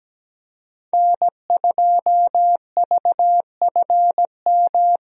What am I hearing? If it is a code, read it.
Morse code, heard as a keyed tone, about 17 wpm, 700 Hz. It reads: N2VFM
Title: USA Call Signs - 5 Characters - Repeated 3x - 17wpm